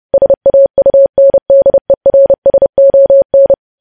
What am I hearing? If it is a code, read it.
Morse code, heard as a keyed tone, about 30 wpm, 550 Hz. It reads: SAUNDERSON
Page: news headlines in morse code at 30 wpm